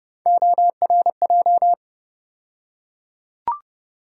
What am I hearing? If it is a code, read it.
Morse code, heard as a keyed tone, about 30 wpm, 700 Hz. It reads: ORJ
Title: USA Call Sign Postfixes - 3 Letters 30wpm